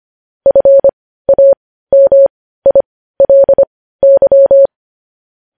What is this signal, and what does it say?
Morse code, heard as a keyed tone, about 25 wpm, spaced as for 15 wpm, 550 Hz. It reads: FAMILY
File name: podcast / News headlines-1x-25x15